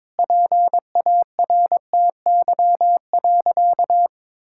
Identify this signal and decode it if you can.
Morse code, heard as a keyed tone, about 22 wpm, 700 Hz. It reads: PARTY.
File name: practice / 22wpm / Sentences with 4-Word Phrases - 22wpm